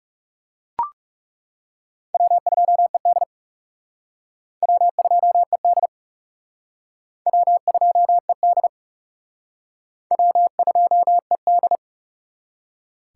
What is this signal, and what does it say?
Morse code, heard as a keyed tone, about 38 wpm, 700 Hz. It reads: W2ED W2ED W2ED W2ED